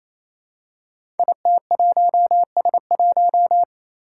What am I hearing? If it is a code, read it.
Morse code, heard as a keyed tone, about 28 wpm, 700 Hz. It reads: IT1S1
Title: Five Challenging Letter-Number Combinations 28wpm